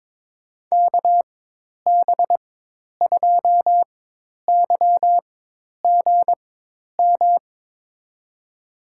Morse code, heard as a keyed tone, about 22 wpm, spaced as for 10 wpm, 700 Hz. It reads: KB2YGM
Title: USA Call Signs - 6 Characters - 4x Character Spacing - 22wpm